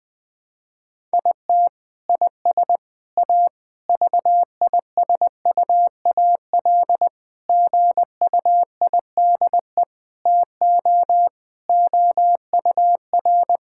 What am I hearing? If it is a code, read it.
Morse code, heard as a keyed tone, about 20 wpm, 700 Hz. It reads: IT IS A VISUAL GUIDE TO OUR
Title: Sentences from Top 1000 Words - Review Part 6 - 20wpm